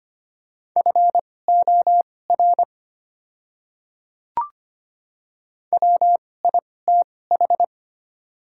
Morse code, heard as a keyed tone, about 25 wpm, spaced as for 18 wpm, 690 Hz. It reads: FOR WITH
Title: 2x Character Spacing - Top 100 Words - 25wpm